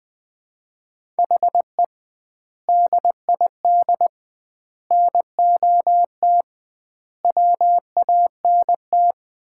Morse code, heard as a keyed tone, about 20 wpm, 700 Hz. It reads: HE DID NOT WANT
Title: 2x Word-Spacing - Sets of 4 Words - 20wpm